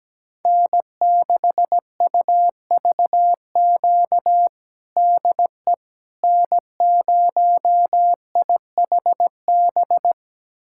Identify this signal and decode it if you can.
Morse code, heard as a keyed tone, about 17 wpm, 700 Hz. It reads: N6UVQ DE N0IHB
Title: USA Callsign DE Callsign - 5 Characters - 17wpm